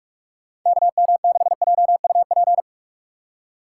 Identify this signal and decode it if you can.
Morse code, heard as a keyed tone, about 44 wpm, 700 Hz. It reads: KM6JUP
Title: USA Call Signs - 6 Characters - 6x Character Spacing - 45wpm